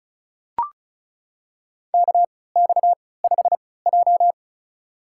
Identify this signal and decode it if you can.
Morse code, heard as a keyed tone, about 34 wpm, spaced as for 20 wpm, 700 Hz. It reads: KX5J
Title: USA Call Signs - 3x Character Spacing - 35wpm